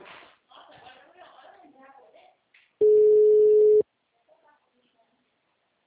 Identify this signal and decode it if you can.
Ringback tone (Europe, 425 Hz), with speech.